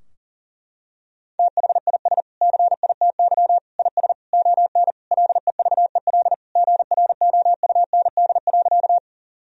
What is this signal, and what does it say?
Morse code, heard as a keyed tone, about 40 wpm, 700 Hz. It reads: THIS CITY IS ON LEVEL GROUND.
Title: Sentences from Top 1000 Words - Review Part 4 - 40wpm